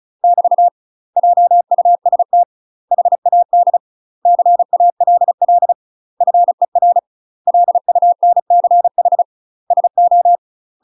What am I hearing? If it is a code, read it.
Morse code, heard as a keyed tone, about 34 wpm, 700 Hz. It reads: <BT> JUST HAD CALL FER LUNCH SO
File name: UK MASTER QSOs TRAINING / 35WPM / UK MASTER QSOs 0001